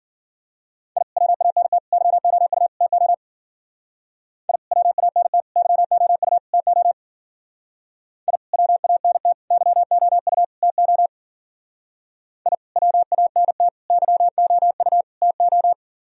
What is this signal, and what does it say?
Morse code, heard as a keyed tone, about 47 wpm, 700 Hz. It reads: I WANT YOU TO I WANT YOU TO I WANT YOU TO I WANT YOU TO
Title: Speed-Racing - Sets of 4 Words 40wpm